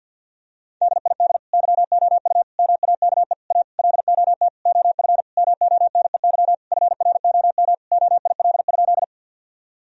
Morse code, heard as a keyed tone, about 50 wpm, 700 Hz. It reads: DID YOU MAKE A LOT OF MONEY FROM OIL?
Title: Sentences from Top 1000 Words - Review Part 4 - 50wpm